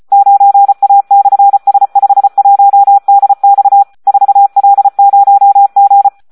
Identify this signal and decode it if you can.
Morse code, heard as a keyed tone, about 34 wpm, 800 Hz. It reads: 9A/S51DX 4L0G
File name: Audio practica CW / SD 1 34